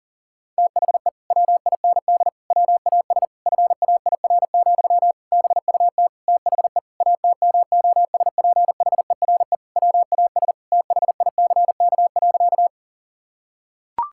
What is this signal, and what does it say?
Morse code, heard as a keyed tone, about 40 wpm, 700 Hz. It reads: THE WIND WAS FAIR, BUT THE ATMOSPHERE WAS THICK.
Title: Sentences from Top 1600 Words 40wpm